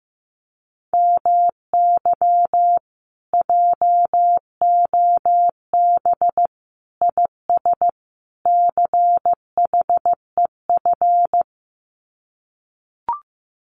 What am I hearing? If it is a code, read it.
Morse code, heard as a keyed tone, about 15 wpm, 700 Hz. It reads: MY JOB IS CHEF
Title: Common QSO Phrases Encore 15wpm